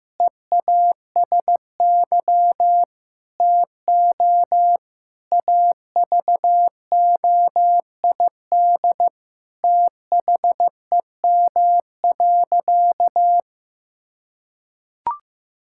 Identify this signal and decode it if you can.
Morse code, heard as a keyed tone, about 15 wpm, 700 Hz. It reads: EASY TO AVOID THEM.
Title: Sentences from Top 1000 Words - Review Part 6 - 15wpm